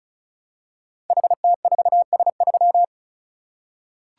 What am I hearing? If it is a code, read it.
Morse code, heard as a keyed tone, about 36 wpm, 700 Hz. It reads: HT4S3
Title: Code Groups - 5 Letter:Numbers - 35wpm